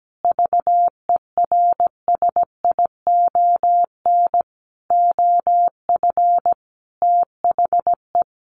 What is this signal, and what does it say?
Morse code, heard as a keyed tone, about 17 wpm, 700 Hz. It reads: VERSION OF THE
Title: Sentences from Top 1400 Words 17wpm